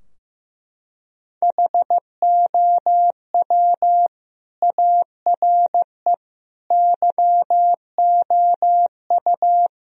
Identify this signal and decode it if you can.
Morse code, heard as a keyed tone, about 16 wpm, 700 Hz. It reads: HOW ARE YOU
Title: Sentences from Top 1000 Words - Review Part 4 - 15wpm